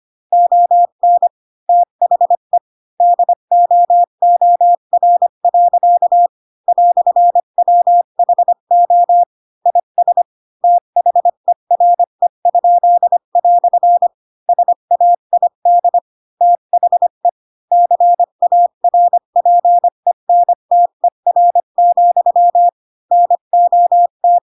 Morse code, heard as a keyed tone, about 25 wpm, 700 Hz. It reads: ON THE DOOR. "WHO IS THERE?" SAID THE CARPENTER, NOT
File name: Pinocchio / Pinocchio ch2 20 25 ik1hge